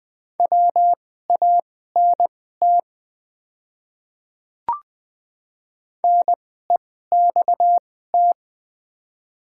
Morse code, heard as a keyed tone, about 20 wpm, spaced as for 15 wpm, 700 Hz. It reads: WANT NEXT
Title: 2x Character Spacing - Top 300 Words - 20wpm